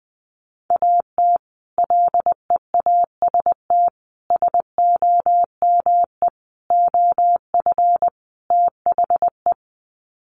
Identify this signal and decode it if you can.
Morse code, heard as a keyed tone, about 20 wpm, 700 Hz. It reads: AT LEAST SOME OF THE
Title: Sets of 5 Words Encore 20wpm